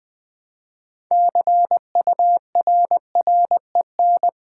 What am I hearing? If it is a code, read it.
Morse code, heard as a keyed tone, about 20 wpm, 700 Hz. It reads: CURREN
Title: Build A Word - 3 Letters At A Time - 8-Letter Words - 20wpm